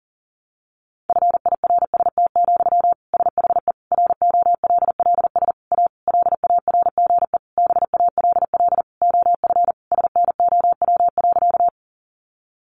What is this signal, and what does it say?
Morse code, heard as a keyed tone, about 40 wpm, 700 Hz. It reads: FIRST, SHE ROLLS A LARGE BALL OF SNOW.
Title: Easy Phrases 40wpm